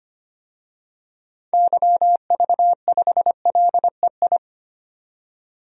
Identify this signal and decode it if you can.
Morse code, heard as a keyed tone, about 25 wpm, 700 Hz. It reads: YV5LEI
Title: DXCC-100 Call Signs - 6 Characters - 25wpm